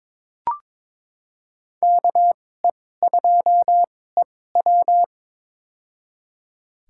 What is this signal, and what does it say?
Morse code, heard as a keyed tone, about 23 wpm, spaced as for 16 wpm, 700 Hz. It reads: KE2EW
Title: USA Call Signs - 2x Character Spacing - 22wpm